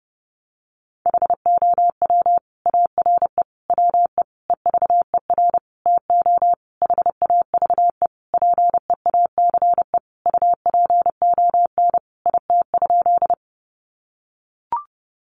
Morse code, heard as a keyed tone, about 30 wpm, 700 Hz. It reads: HOW ARE WE EVER TO HAVE PEACE UPON IT?
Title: Sentences from Top 1500 Words 30wpm